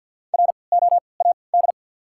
Morse code, heard as a keyed tone, about 53 wpm, spaced as for 29 wpm, 700 Hz. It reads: ROAD
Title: 3x Character Spacing - Top 500 Words - 50wpm